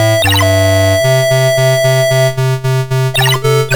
There is no code.